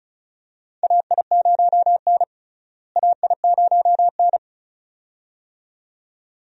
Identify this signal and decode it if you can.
Morse code, heard as a keyed tone, about 34 wpm, 700 Hz. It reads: AI0N AI0N
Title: USA Call Signs - 4 Characters - Repeated 2x - 35wpm